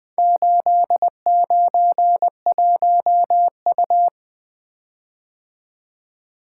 Morse code, heard as a keyed tone, about 20 wpm, 700 Hz. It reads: 891U
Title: Four Letter-Number Combinations - 20wpm